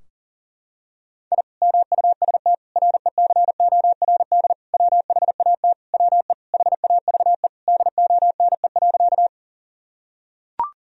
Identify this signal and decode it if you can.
Morse code, heard as a keyed tone, about 37 wpm, 700 Hz. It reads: I MUST RECORD WHAT WE HAVE DONE.
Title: Sentences from Top 1000 Words - Review Part 3 - 40wpm